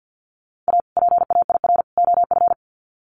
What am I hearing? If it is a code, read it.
Morse code, heard as a keyed tone, about 50 wpm, 700 Hz. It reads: A PAIR OF
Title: Sets of 3 Words 50wpm